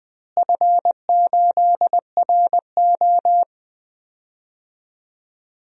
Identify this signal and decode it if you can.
Morse code, heard as a keyed tone, about 20 wpm, 700 Hz. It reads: F8RO
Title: DXCC-100 Call Signs - 4 Characters - Repeated 3x - 20wpm